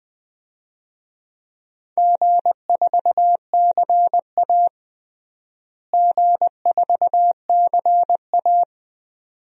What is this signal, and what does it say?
Morse code, heard as a keyed tone, about 20 wpm, 700 Hz. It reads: G4CA G4CA
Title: DXCC-100 Call Signs - 4 Characters - Repeated 2x - 20wpm